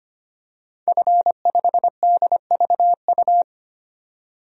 Morse code, heard as a keyed tone, about 25 wpm, 700 Hz. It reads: F5DVU
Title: DXCC-100 Call Signs - 5 Characters - 25wpm